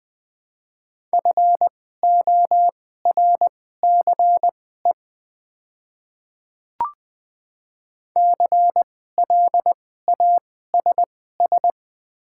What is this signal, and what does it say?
Morse code, heard as a keyed tone, about 20 wpm, spaced as for 14 wpm, 700 Hz. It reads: FORCE CLASS
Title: 2x Character Spacing - Top 500 Words - 20wpm